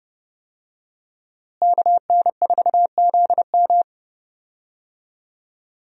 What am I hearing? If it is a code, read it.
Morse code, heard as a keyed tone, about 32 wpm, 700 Hz. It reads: KN4ZM
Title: USA Call Signs - 5 Characters - Repeated 2x - 30wpm